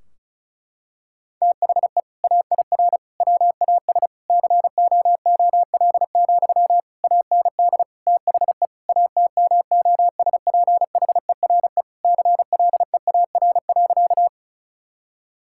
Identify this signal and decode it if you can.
Morse code, heard as a keyed tone, about 36 wpm, 700 Hz. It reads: THE AIR WAS COOL, AND THE ATMOSPHERE CLEAR.
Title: Sentences from Top 1600 Words 35wpm